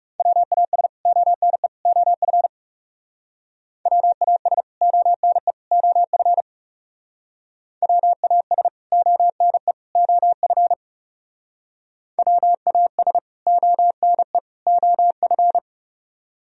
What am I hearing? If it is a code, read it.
Morse code, heard as a keyed tone, about 38 wpm, 700 Hz. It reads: WAS ONE OF WAS ONE OF WAS ONE OF WAS ONE OF